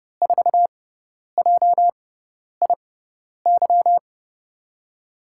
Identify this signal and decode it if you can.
Morse code, heard as a keyed tone, about 32 wpm, spaced as for 10 wpm, 700 Hz. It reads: 4JIY
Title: USA Call Signs - 6 Characters - 6x Character Spacing - 30wpm